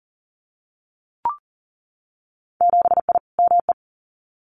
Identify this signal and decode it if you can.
Morse code, heard as a keyed tone, about 40 wpm, 700 Hz. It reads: 7I ME
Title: Winter Field Day Exchanges Encore 40wpm